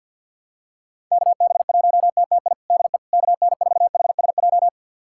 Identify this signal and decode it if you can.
Morse code, heard as a keyed tone, about 50 wpm, 700 Hz. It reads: KB1TTI DE KN4HSJ
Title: USA Callsign DE Callsign 50wpm